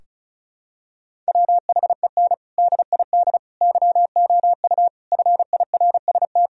Morse code, heard as a keyed tone, about 34 wpm, 700 Hz. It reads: WHEN DID YOU FIRST
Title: Sentences from Top 1000 Words - Review Part 2 - 35wpm